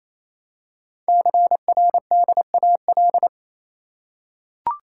Morse code, heard as a keyed tone, about 27 wpm, 700 Hz. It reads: CRDAL
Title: Code Groups - 5 Letters - 28wpm